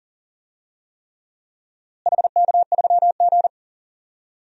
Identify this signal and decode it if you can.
Morse code, heard as a keyed tone, about 40 wpm, 700 Hz. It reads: HK3G
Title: DXCC-100 Call Signs - 4 Characters - Repeated 3x - 40wpm